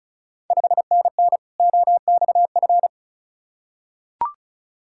Morse code, heard as a keyed tone, about 36 wpm, 700 Hz. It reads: 5NN OXF
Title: Ontario QSO Party Exchange 35wpm